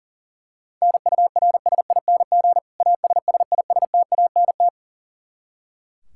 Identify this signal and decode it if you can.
Morse code, heard as a keyed tone, about 37 wpm, 700 Hz. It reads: NURSING ASSISTANT
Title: Practice QSO - Exchange 3 - More Info - 40wpm